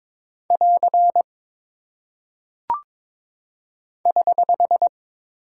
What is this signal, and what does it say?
Morse code, heard as a keyed tone, about 22 wpm, 700 Hz. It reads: <AR> <HH>
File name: practice / 22wpm / Single Punctuation-Prosign - 22wpm